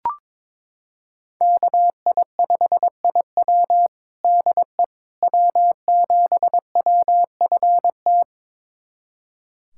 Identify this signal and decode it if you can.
Morse code, heard as a keyed tone, about 22 wpm, 700 Hz. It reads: KI5IW DE W7WFT
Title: USA Callsign DE Callsign 22wpm